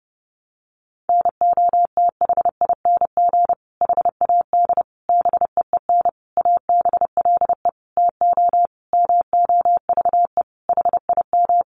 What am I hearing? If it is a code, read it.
Morse code, heard as a keyed tone, about 28 wpm, 700 Hz. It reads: NOTHING HAD BEEN ABLE TO MOVE HIM
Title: Sentences from Top 1300 Words 30wpm